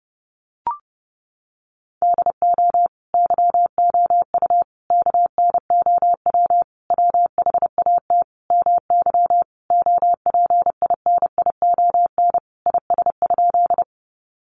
Morse code, heard as a keyed tone, about 30 wpm, 700 Hz. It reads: DO YOU KNOW WHAT MY OPINION IS?
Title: Sentences from Top 1200 Words 30wpm